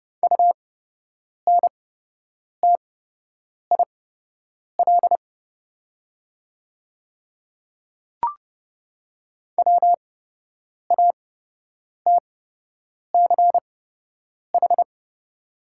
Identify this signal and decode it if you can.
Morse code, heard as a keyed tone, about 30 wpm, spaced as for 8 wpm, 700 Hz. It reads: UNTIL WATCH
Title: 8x Character Spacing - Top 300 Words - 30wpm